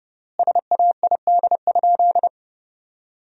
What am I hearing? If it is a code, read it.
Morse code, heard as a keyed tone, about 32 wpm, 700 Hz. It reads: SAID?
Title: 3x Word-Spacing - Sentences from Top 100 Words - 30wpm